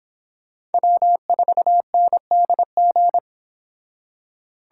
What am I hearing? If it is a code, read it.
Morse code, heard as a keyed tone, about 28 wpm, 700 Hz. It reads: W4NDG